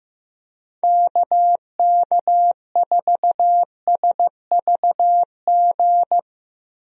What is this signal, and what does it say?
Morse code, heard as a keyed tone, about 15 wpm, 700 Hz. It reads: KK4SVG
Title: USA Call Signs - 6 Characters - Repeated 3x - 15wpm